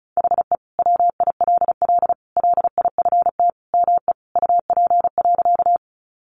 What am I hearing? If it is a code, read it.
Morse code, heard as a keyed tone, about 34 wpm, 700 Hz. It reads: HE WILL LIFT ME UP.
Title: Sentences from Top 1000 Words 35wpm